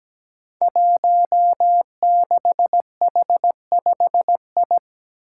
Morse code, heard as a keyed tone, about 17 wpm, 700 Hz. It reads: 16H5I
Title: Five Challenging Letter-Number Combinations 17wpm